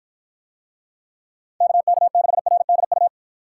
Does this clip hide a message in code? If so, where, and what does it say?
Yes: Morse code from 1.6 to 3.1 s, heard as a keyed tone, about 50 wpm, 700 Hz. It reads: KK6RDU